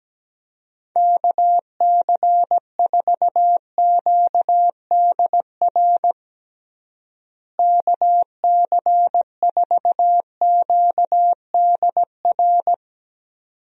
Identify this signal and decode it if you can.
Morse code, heard as a keyed tone, about 17 wpm, 700 Hz. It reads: KC4QDR KC4QDR
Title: USA Call Signs - 6 Characters - Repeated 2x - 17wpm